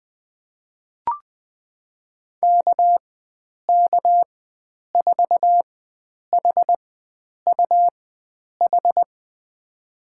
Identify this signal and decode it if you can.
Morse code, heard as a keyed tone, about 20 wpm, spaced as for 9 wpm, 700 Hz. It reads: KK4HUH
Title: USA Call Signs - 6 Characters - 4x Character Spacing - 20wpm